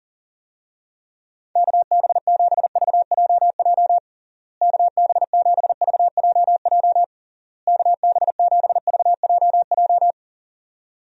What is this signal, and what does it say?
Morse code, heard as a keyed tone, about 40 wpm, 700 Hz. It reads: KB7VJJ KB7VJJ KB7VJJ
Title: USA Call Signs - 6 Characters - Repeated 3x - 40wpm